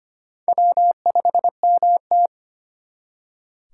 Morse code, heard as a keyed tone, about 33 wpm, 700 Hz. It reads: W5MT